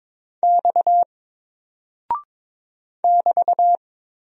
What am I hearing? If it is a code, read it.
Morse code, heard as a keyed tone, about 22 wpm, 700 Hz. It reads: X <BT>
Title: Single Character (Letter:Number:Punctuation:Prosign) 22wpm